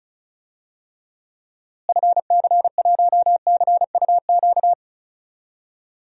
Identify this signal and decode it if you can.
Morse code, heard as a keyed tone, about 36 wpm, 700 Hz. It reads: FC1CUQ
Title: DXCC-100 Call Signs - 6 Characters - Repeated 3x - 35wpm